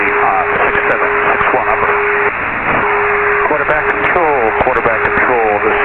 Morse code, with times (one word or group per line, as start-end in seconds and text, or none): none